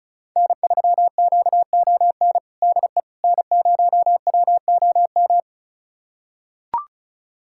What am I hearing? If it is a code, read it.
Morse code, heard as a keyed tone, about 34 wpm, 700 Hz. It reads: N3QON DE N0WOM
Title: USA Callsign DE Callsign - 5 Characters - 35wpm